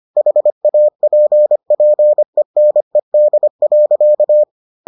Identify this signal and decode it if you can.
Morse code, heard as a keyed tone, about 25 wpm, 600 Hz. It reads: HAPPENED.